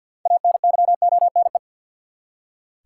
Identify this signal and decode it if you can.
Morse code, heard as a keyed tone, about 48 wpm, 700 Hz. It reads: ANYONE
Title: Six Letter Words - 50wpm